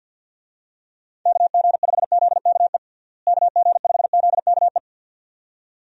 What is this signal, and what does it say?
Morse code, heard as a keyed tone, about 49 wpm, 700 Hz. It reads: KG5ZKE KG5ZKE
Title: USA Call Signs - 6 Characters - Repeated 2x - 50wpm